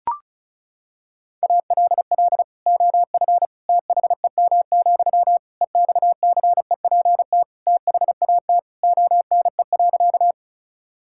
Morse code, heard as a keyed tone, about 34 wpm, 700 Hz. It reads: ALL OF THEM, EXCEPT THAT ONE.
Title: Sentences from Top 1600 Words 35wpm